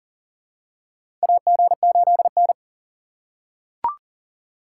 Morse code, heard as a keyed tone, about 40 wpm, 700 Hz. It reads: AG8N
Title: USA Call Signs - 4 Characters - 2x Character Spacing - 40wpm